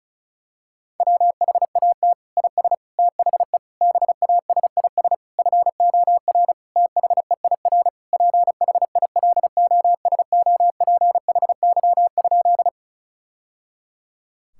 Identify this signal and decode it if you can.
Morse code, heard as a keyed tone, about 36 wpm, 700 Hz. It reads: WHAT IS THE BASIS FOR THEIR PHILOSOPHY?
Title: Sentences from Top 1000 Words - Review Part 6 - 35wpm